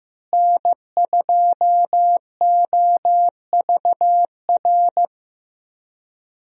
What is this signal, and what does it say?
Morse code, heard as a keyed tone, about 15 wpm, 700 Hz. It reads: N2OVR
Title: USA Call Signs - 5 Characters - Repeated 3x - 15wpm